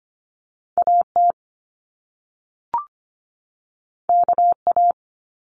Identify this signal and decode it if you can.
Morse code, heard as a keyed tone, about 25 wpm, 700 Hz. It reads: AT KA
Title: Top Two Letter Combinations in Words 25wpm